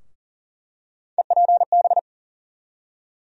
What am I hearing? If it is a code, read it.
Morse code, heard as a keyed tone, about 39 wpm, 700 Hz. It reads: EPB
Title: USA Call Sign Postfixes - 3 Letters 40wpm